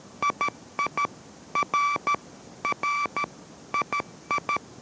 Morse code, heard as a keyed tone, about 13 wpm, 1160 Hz. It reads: II R R II